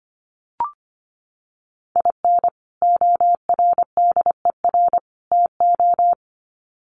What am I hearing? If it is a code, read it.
Morse code, heard as a keyed tone, about 25 wpm, 700 Hz. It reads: IN ORDER TO
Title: Sets of 3 Words 25wpm